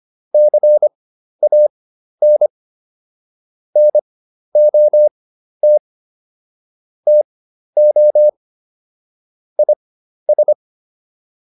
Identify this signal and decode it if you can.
Morse code, heard as a keyed tone, about 25 wpm, spaced as for 12 wpm, 600 Hz. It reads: CAN NOT TO IS